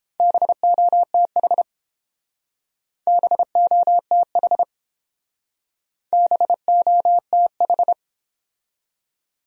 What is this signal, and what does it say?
Morse code, heard as a keyed tone, about 29 wpm, 700 Hz. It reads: BOTH BOTH BOTH